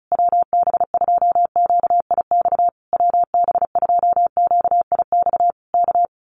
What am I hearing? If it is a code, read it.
Morse code, heard as a keyed tone, about 34 wpm, 700 Hz. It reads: WB2QIX WB2QIX K
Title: Common QSO Phrases Encore 35wpm